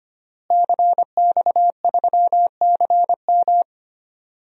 Morse code, heard as a keyed tone, about 25 wpm, 700 Hz. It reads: CX3CM
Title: DXCC-100 Call Signs - 5 Characters - Repeated 3x - 25wpm